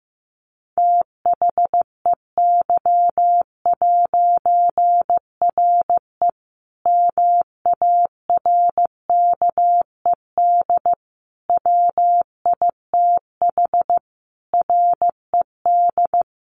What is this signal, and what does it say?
Morse code, heard as a keyed tone, about 15 wpm, 700 Hz. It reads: THEY'RE MARKED WITH RED
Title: A Sound Of Thunder 15wpm